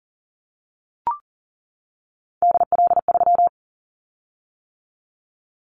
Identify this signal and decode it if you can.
Morse code, heard as a keyed tone, about 42 wpm, 700 Hz. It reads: DL3
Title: Callsign Prefixes for Top 50 DXCC - 40wpm